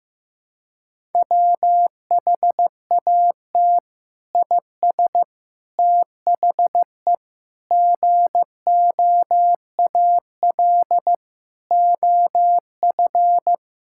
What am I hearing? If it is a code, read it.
Morse code, heard as a keyed tone, about 15 wpm, 700 Hz. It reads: WHAT IS THE GOAL OF
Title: Sentences from Top 1000 Words - Review Part 5 - 15wpm